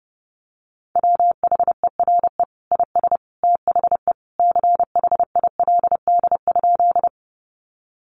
Morse code, heard as a keyed tone, about 30 wpm, 700 Hz. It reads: WHERE IS THE CHILD?
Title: Sentences from Top 800 Words 30wpm